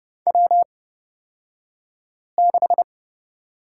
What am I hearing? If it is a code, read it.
Morse code, heard as a keyed tone, about 30 wpm, spaced as for 5 wpm, 700 Hz. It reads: W6
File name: Lesson 38d - Letter q - Two Characters 30wpm